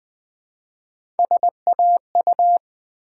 Morse code, heard as a keyed tone, about 20 wpm, 700 Hz. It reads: SAU
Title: Build A Word - 1 Letter At A Time - 5-Letter Words - 20wpm